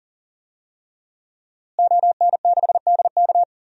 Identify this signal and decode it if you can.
Morse code, heard as a keyed tone, about 40 wpm, 700 Hz. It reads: ON6DK
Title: DXCC-100 Call Signs - 5 Characters - Repeated 3x - 40wpm